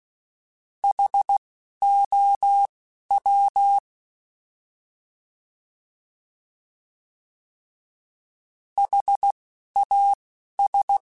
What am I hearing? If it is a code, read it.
Morse code, heard as a keyed tone, about 16 wpm, spaced as for 11 wpm, 800 Hz. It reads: HOW HAS
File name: Alphabetical / Alphabetical mp3s / Hs @10